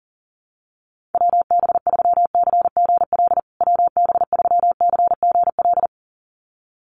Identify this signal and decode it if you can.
Morse code, heard as a keyed tone, about 40 wpm, 700 Hz. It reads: WB3CGL WB3CGL
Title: USA Call Signs Repeated 2x 40wpm